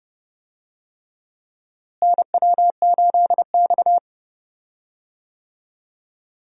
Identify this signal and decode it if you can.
Morse code, heard as a keyed tone, about 30 wpm, 700 Hz. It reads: NW8X
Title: USA Call Signs - 4 Characters - Repeated 3x - 30wpm